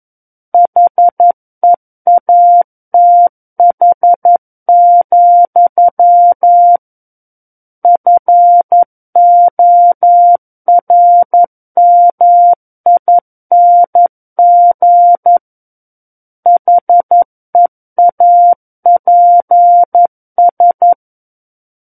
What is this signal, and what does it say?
Morse code, heard as a keyed tone, about 11 wpm, 700 Hz. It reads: HEATH, FORMING HEAPS
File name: Book One / 02-Chapter 2 - 11 WPM